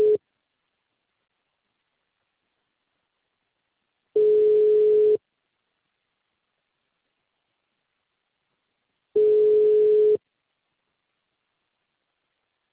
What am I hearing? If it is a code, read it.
Ringback tone (Europe, 425 Hz).